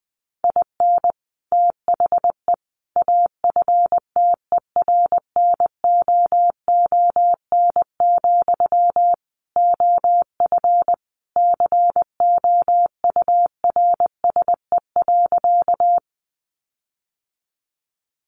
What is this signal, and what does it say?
Morse code, heard as a keyed tone, about 19 wpm, 700 Hz. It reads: IN THE AFTERNOON, OF COURSE.
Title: Sentences from Top 1500 Words 20wpm